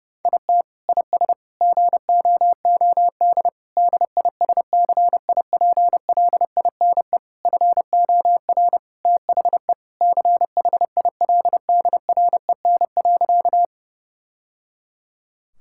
Morse code, heard as a keyed tone, about 30 wpm, 700 Hz. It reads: IT IS GOOD DISCIPLINE FOR THE CHILDREN.
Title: Sentences from Top 1000 Words - Review Part 6 - 30wpm